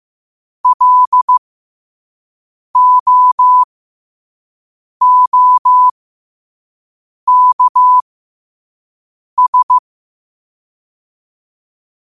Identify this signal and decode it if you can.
Morse code, heard as a keyed tone, about 15 wpm, spaced as for 5 wpm, 1000 Hz. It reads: LOOKS